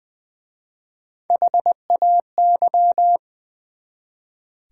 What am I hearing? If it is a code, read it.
Morse code, heard as a keyed tone, about 20 wpm, 700 Hz. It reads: HAY
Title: Easy Words with Hints - 20wpm